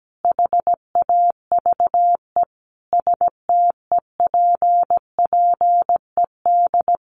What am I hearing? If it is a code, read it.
Morse code, heard as a keyed tone, about 17 wpm, 700 Hz. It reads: HAVE STEPPED
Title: A Sound Of Thunder 17wpm